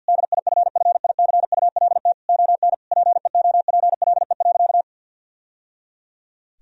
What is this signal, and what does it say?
Morse code, heard as a keyed tone, about 50 wpm, 700 Hz. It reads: DIFFICULT ON PEOPLE.
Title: Sentences from Top 1000 Words - Review Part 3 - 50wpm